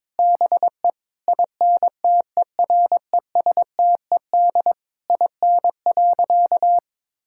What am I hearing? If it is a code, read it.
Morse code, heard as a keyed tone, about 22 wpm, 700 Hz. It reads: BE INTERESTED IN.
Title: Sentences from Top 1000 Words - Review Part 6 - 22wpm